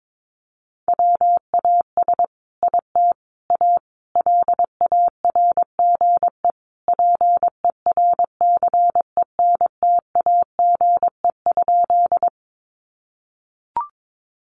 Morse code, heard as a keyed tone, about 22 wpm, 700 Hz. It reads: WAS IT A LARGE PERCENTAGE?
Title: Sentences from Top 1300 Words 22wpm